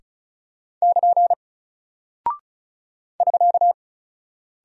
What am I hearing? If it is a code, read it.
Morse code, heard as a keyed tone, about 36 wpm, 700 Hz. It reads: <KN> <SK>
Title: Single Character (Letter:Number:Punctuation:Prosign) Alt 35wpm